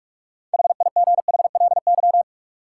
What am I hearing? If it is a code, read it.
Morse code, heard as a keyed tone, about 44 wpm, 700 Hz. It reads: HIGHLY
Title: Six Letter Words - 45wpm